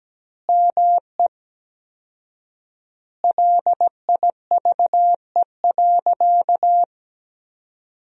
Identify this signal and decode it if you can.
Morse code, heard as a keyed tone, about 17 wpm, 700 Hz. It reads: ME LIVE.
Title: 4x Word-Spacing - Sentences from Top 300 Words - 17wpm